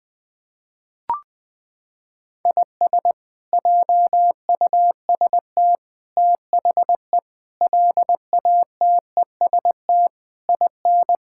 Morse code, heard as a keyed tone, about 21 wpm, 700 Hz. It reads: IS JUST THE LATEST IN
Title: Sets of 8 Words - 20wpm